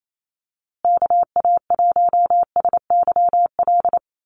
Morse code, heard as a keyed tone, about 28 wpm, 700 Hz. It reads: KA1SYL
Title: USA Call Signs Repeated 3x 28wpm